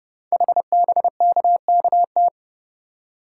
Morse code, heard as a keyed tone, about 30 wpm, 700 Hz. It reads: HBKKT
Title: Code Groups - 5 Letters - 30wpm